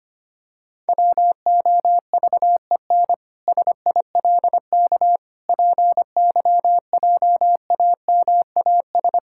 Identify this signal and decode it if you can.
Morse code, heard as a keyed tone, about 25 wpm, 700 Hz. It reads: WOVEN SILK PYJAMAS
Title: Pangram Challenge v2 - 25wpm